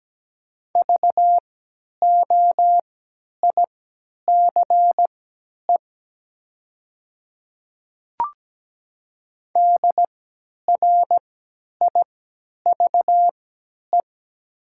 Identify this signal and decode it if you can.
Morse code, heard as a keyed tone, about 17 wpm, spaced as for 10 wpm, 700 Hz. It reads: VOICE DRIVE
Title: 3x Character Spacing - Top 500 Words - 17wpm